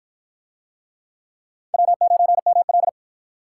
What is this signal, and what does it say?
Morse code, heard as a keyed tone, about 53 wpm, 700 Hz. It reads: W9ML